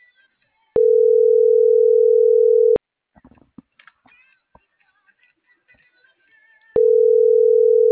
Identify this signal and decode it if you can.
Ringback tone (North America), 440 + 480 Hz.